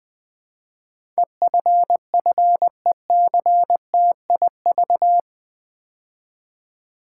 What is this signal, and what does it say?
Morse code, heard as a keyed tone, about 20 wpm, 700 Hz. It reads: EFFECTIV
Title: Build A Word - 4 Letters At A Time - 11-Letter Words - 20wpm